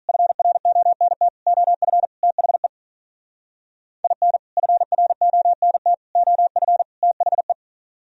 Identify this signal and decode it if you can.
Morse code, heard as a keyed tone, about 45 wpm, 700 Hz. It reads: FRONT OF THE IN FRONT OF THE